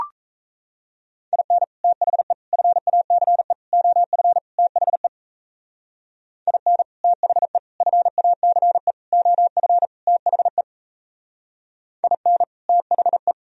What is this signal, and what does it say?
Morse code, heard as a keyed tone, about 35 wpm, 700 Hz. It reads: IN THE FACE OF THE IN THE FACE OF THE IN THE
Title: Speed-Racing - Sets of 5 Words 28wpm